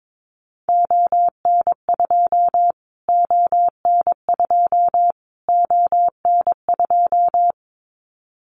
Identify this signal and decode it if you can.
Morse code, heard as a keyed tone, about 22 wpm, 700 Hz. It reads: ON2 ON2 ON2
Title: Callsign Prefixes for Top 25 DXCC - Repeated 3x - 22wpm